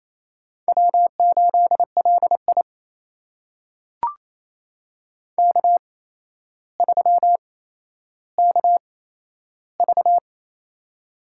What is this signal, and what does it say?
Morse code, heard as a keyed tone, about 28 wpm, spaced as for 12 wpm, 700 Hz. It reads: W8LI K3KV
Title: USA Call Signs - 8x Character Spacing - 28wpm